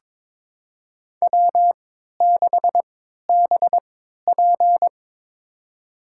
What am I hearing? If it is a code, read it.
Morse code, heard as a keyed tone, about 22 wpm, spaced as for 13 wpm, 700 Hz. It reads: W6BP
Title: USA Call Signs - 3x Character Spacing - 22wpm